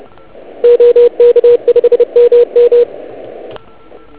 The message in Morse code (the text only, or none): OK5MM